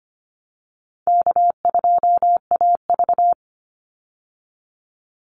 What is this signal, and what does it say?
Morse code, heard as a keyed tone, about 25 wpm, 700 Hz. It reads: K2AV
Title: Easy Call Signs Repeated 2x - 25wpm